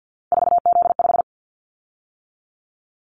Morse code, heard as a keyed tone, about 50 wpm, 700 Hz. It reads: 4Z5
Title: Callsign Prefixes for Top 50 DXCC - 50wpm